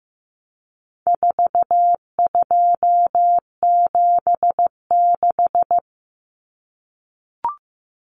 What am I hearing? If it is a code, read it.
Morse code, heard as a keyed tone, about 14 wpm, 700 Hz. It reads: 4276
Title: Common Cut Numbers - 4-Digit 15wpm